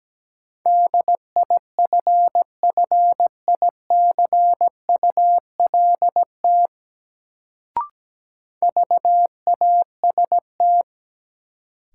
Morse code, heard as a keyed tone, about 17 wpm, 700 Hz. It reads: DIFFICULT VAST
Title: Sets of 4 Words - New Word Drill - 17wpm